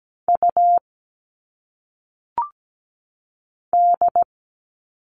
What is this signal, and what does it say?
Morse code, heard as a keyed tone, about 17 wpm, 700 Hz. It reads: U D
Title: All Cut Numbers - 1-Digit 17wpm